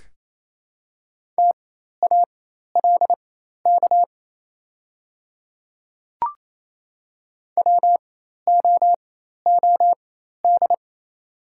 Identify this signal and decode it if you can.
Morse code, heard as a keyed tone, about 28 wpm, spaced as for 13 wpm, 700 Hz. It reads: TALK WOOD
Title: 4x Character Spacing - Top 500 Words - 28wpm